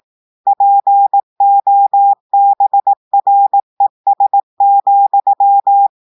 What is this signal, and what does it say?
Morse code, heard as a keyed tone, about 18 wpm, 800 Hz. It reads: POBRES,
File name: audios / portugais / portugais texte en clair 1 vitesse 18 mots minutes